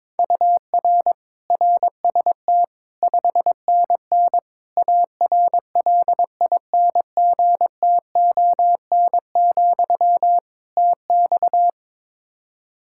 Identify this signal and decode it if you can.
Morse code, heard as a keyed tone, about 22 wpm, 700 Hz. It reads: UR RST 5NN ARLINGTON, TX
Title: Practice QSO - Exchange 1 - Beginning - 22wpm